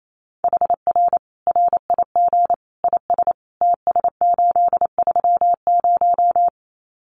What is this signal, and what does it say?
Morse code, heard as a keyed tone, about 28 wpm, 700 Hz. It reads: HR RIG IS TS830
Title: Common QSO Phrases 28wpm